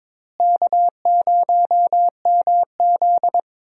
Morse code, heard as a keyed tone, about 22 wpm, 700 Hz. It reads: K0MZ
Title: USA Call Signs - 4 Characters - Repeated 3x - 22wpm